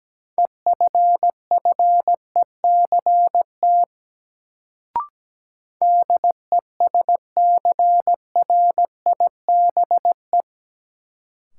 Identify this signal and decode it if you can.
Morse code, heard as a keyed tone, about 17 wpm, 700 Hz. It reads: EFFECT DESCRIBE
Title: Sentences from Top 1000 Words - New Word Drill - 17wpm